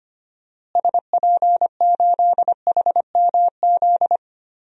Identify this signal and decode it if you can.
Morse code, heard as a keyed tone, about 25 wpm, 700 Hz. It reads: SP8HMZ
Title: DXCC-100 Call Signs - 6 Characters - Repeated 3x - 25wpm